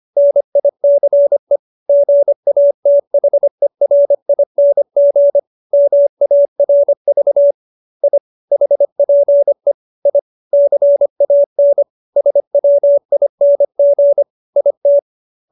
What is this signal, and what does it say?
Morse code, heard as a keyed tone, about 25 wpm, 570 Hz. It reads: NICE GATHERING MARV I HPE I CAN SWING IT